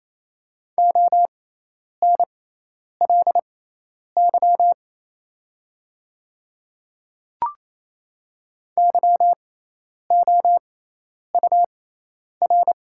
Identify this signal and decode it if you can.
Morse code, heard as a keyed tone, about 28 wpm, spaced as for 10 wpm, 700 Hz. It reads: ONLY YOUR
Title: 6x Character Spacing - Top 300 Words - 28wpm